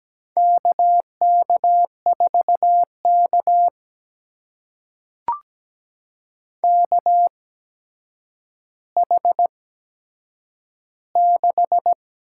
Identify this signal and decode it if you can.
Morse code, heard as a keyed tone, about 17 wpm, spaced as for 8 wpm, 700 Hz. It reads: KK4K KH6
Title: USA Call Signs - 4 Characters - 8x Character Spacing - 17wpm